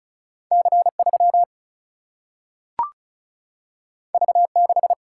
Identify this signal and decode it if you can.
Morse code, heard as a keyed tone, about 36 wpm, 700 Hz. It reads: C 3 V 6
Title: Single-Letter Single-Number 35wpm